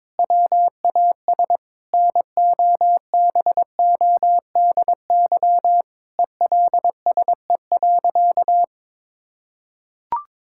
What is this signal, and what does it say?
Morse code, heard as a keyed tone, about 22 wpm, 700 Hz. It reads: WAS NOBODY ELSE.
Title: Sentences from Top 1600 Words 22wpm